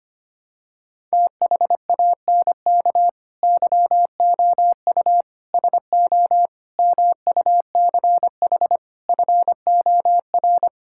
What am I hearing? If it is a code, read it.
Morse code, heard as a keyed tone, about 25 wpm, 700 Hz. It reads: THANK YOU SO MUCH FOR
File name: Sets of 9 Words - 25wpm